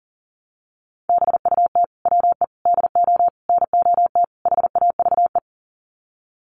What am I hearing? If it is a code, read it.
Morse code, heard as a keyed tone, about 40 wpm, 700 Hz. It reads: BUT WE DO NOT HAVE
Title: Sets of 5 Words Encore2 40wpm